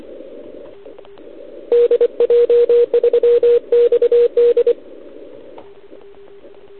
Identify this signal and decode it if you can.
Morse code, heard as a keyed tone, about 25 wpm, 480 Hz. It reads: DJ3XD
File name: ARCHIV / OKDX2019 / OKOM40 KX3